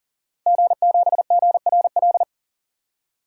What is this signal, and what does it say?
Morse code, heard as a keyed tone, about 39 wpm, 700 Hz. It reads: G7GRL
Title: DXCC-100 Call Signs - 5 Characters - Repeated 2x - 40wpm